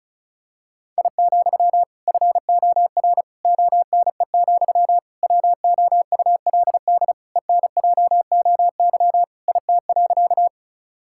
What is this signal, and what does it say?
Morse code, heard as a keyed tone, about 36 wpm, 700 Hz. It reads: I, FOR ONE, WOULD ENJOY IT.
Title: Sentences from Top 1000 Words - Review Part 5 - 35wpm